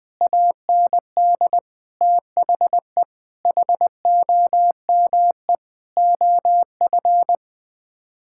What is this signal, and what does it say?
Morse code, heard as a keyed tone, about 20 wpm, 700 Hz. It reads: AND THE HOME OF
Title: Sets of 9 Words - 20wpm